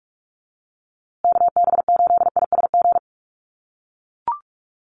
Morse code, heard as a keyed tone, about 44 wpm, 700 Hz. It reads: KB8ISG
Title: USA Call Signs Repeated 2x 45wpm